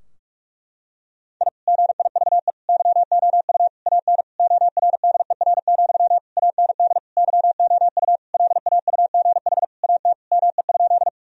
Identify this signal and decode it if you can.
Morse code, heard as a keyed tone, about 46 wpm, 700 Hz. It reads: I GIVE YOU AN ORDER, AND YOU LAUGH AT ME?
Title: Sentences from Top 1000 Words - Review Part 2 - 45wpm